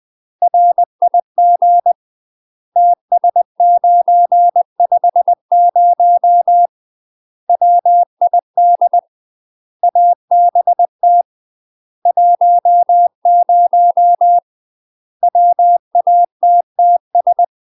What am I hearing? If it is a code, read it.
Morse code, heard as a keyed tone, about 20 wpm, 700 Hz. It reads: RIG TS950 WID ABT 10 WATTS
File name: UK MASTER QSOs TRAINING / 20WPM / UK MASTER QSOs 0064